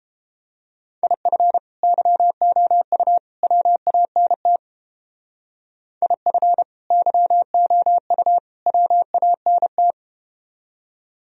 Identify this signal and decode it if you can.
Morse code, heard as a keyed tone, about 31 wpm, 700 Hz. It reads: IF YOU WANT IF YOU WANT